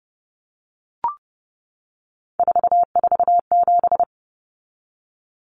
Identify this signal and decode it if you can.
Morse code, heard as a keyed tone, about 30 wpm, 700 Hz. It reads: 447
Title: Common Cut Numbers - 3-Digit 30wpm